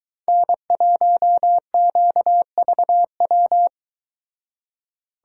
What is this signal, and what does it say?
Morse code, heard as a keyed tone, about 23 wpm, 700 Hz. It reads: N1QVW